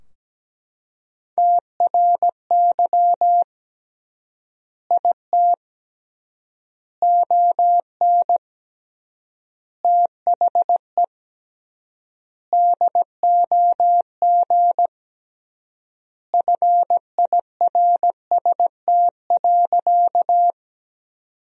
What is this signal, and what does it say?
Morse code, heard as a keyed tone, about 17 wpm, 700 Hz. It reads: TRY IT ON THE DOG FIRST.
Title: 3x Word-Spacing - Sentences from Top 400 Words - 17wpm